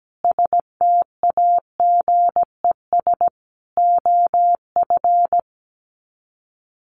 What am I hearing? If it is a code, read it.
Morse code, heard as a keyed tone, about 17 wpm, 700 Hz. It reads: STAGES OF
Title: Sets of 5 Words Encore 17wpm